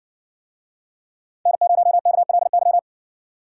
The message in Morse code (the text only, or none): N0KLY